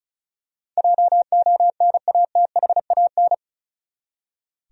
Morse code, heard as a keyed tone, about 36 wpm, 700 Hz. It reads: JONATHAN
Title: USA Common Names - 35wpm